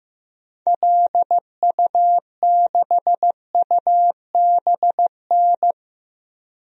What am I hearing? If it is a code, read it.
Morse code, heard as a keyed tone, about 15 wpm, 700 Hz. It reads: LU6UBN
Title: DXCC-100 Call Signs - 6 Characters - 15wpm